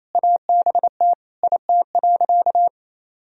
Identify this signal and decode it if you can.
Morse code, heard as a keyed tone, about 28 wpm, 700 Hz. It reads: ABT IT.
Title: Sentences from Top 400 Words and Abbreviations 28wpm